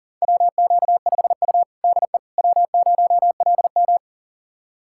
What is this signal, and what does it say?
Morse code, heard as a keyed tone, about 37 wpm, 700 Hz. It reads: WQ5U DE W0LM
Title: USA Callsign DE Callsign - 4 Characters - 40wpm